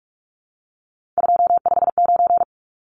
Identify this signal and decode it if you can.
Morse code, heard as a keyed tone, about 44 wpm, 700 Hz. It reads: 259
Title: Numbers - 3-Digit 45wpm